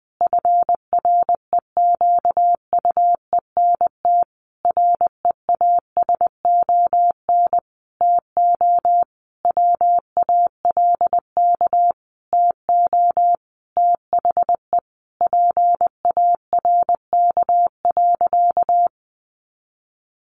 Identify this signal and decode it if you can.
Morse code, heard as a keyed tone, about 20 wpm, 700 Hz. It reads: FREQUENT REASON TO WALK TO THE PARK.
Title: Sentences from Top 1400 Words 20wpm